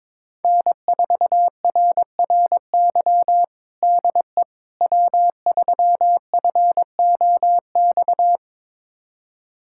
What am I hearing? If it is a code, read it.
Morse code, heard as a keyed tone, about 22 wpm, 700 Hz. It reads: N4RRY DE W3FOX
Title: USA Callsign DE Callsign - 5 Characters - 22wpm